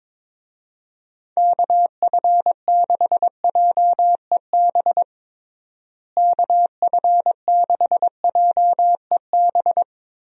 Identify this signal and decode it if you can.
Morse code, heard as a keyed tone, about 22 wpm, 700 Hz. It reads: KF6JEB KF6JEB
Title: USA Call Signs - 6 Characters - Repeated 2x - 22wpm